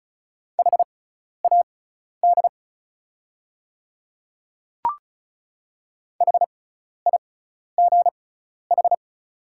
Morse code, heard as a keyed tone, about 36 wpm, spaced as for 12 wpm, 690 Hz. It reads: HAD HIGH
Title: 6x Character Spacing - Top 300 Words - 35wpm